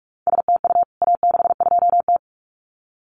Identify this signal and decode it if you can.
Morse code, heard as a keyed tone, about 44 wpm, 700 Hz. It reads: STU A62T
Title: CWT Exchange 45wpm